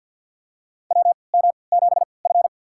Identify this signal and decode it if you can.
Morse code, heard as a keyed tone, about 50 wpm, spaced as for 28 wpm, 700 Hz. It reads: WM7F
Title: USA Call Signs - 4 Characters - 3x Character Spacing - 50wpm